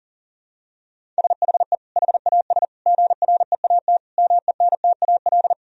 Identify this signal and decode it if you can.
Morse code, heard as a keyed tone, about 40 wpm, 700 Hz. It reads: SHE HAS GREAT MENTAL
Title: Sentences from Top 1000 Words - Review Part 4 - 40wpm